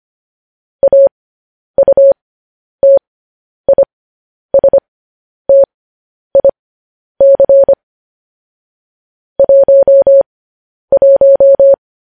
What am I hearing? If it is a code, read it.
Morse code, heard as a keyed tone, about 25 wpm, spaced as for 10 wpm, 550 Hz. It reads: AUTISTIC 11
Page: news headlines in morse code at 10 wpm